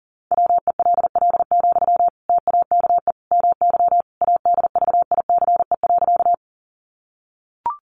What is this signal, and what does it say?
Morse code, heard as a keyed tone, about 40 wpm, 700 Hz. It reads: WELL, TAKE MY ADVICE.
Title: Sentences from Top 1100 Words 40wpm